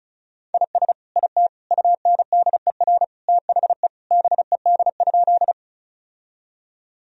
Morse code, heard as a keyed tone, about 36 wpm, 690 Hz. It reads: IS IT UNDER THE BED?
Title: Sentences from Top 1000 Words - Review Part 1 - 35wpm